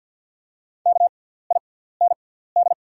Morse code, heard as a keyed tone, about 48 wpm, spaced as for 17 wpm, 690 Hz. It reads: KIND
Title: 6x Character Spacing - Top 200 Words - 50wpm